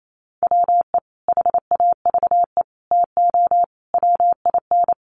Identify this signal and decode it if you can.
Morse code, heard as a keyed tone, about 28 wpm, 700 Hz. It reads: WE HAVE TO WIN
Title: Sentences from Top 1500 Words 28wpm